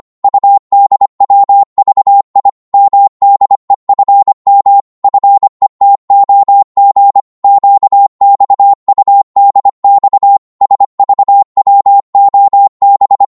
Morse code, heard as a keyed tone, about 25 wpm, 800 Hz. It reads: UDWVI MDEFM FETOG QXUDX SVWOB